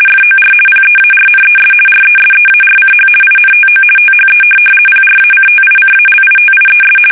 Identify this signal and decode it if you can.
RTTY, about 50 baud. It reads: FOR PEARNES RESEARCH INTO THE PRACTICALITIES
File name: fsk850-50B / teletype-corp